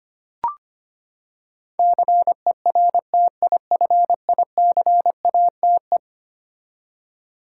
Morse code, heard as a keyed tone, about 25 wpm, 700 Hz. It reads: CERTIFICATE
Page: morse-code-ninja-practice-25wpm